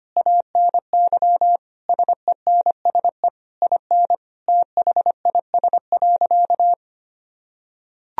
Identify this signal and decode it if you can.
Morse code, heard as a keyed tone, about 25 wpm, 700 Hz. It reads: ANY SENSE IN THIS.
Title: Sentences from Top 1000 Words - Review Part 4 - 25wpm